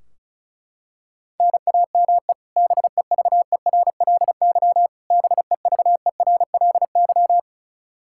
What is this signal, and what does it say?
Morse code, heard as a keyed tone, about 36 wpm, 700 Hz. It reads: NAME BEVERLY BEVERLY
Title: Practice QSO - Exchange 1 - Beginning - 35wpm